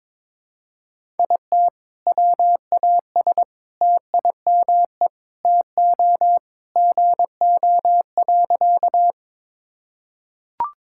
Morse code, heard as a keyed tone, about 22 wpm, 700 Hz. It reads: IT WAS TIME TO GO.
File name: practice / 22wpm / Sentences with 4-Word Phrases - 22wpm